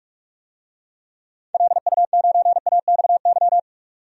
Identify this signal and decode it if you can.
Morse code, heard as a keyed tone, about 43 wpm, 700 Hz. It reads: LU9AXY
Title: DXCC-100 Call Signs - 6 Characters - 45wpm